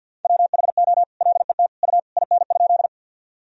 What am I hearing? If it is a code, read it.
Morse code, heard as a keyed tone, about 50 wpm, 700 Hz. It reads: WHO LET U IN?
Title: Sentences from Top 300 Words and Abbreviations 50wpm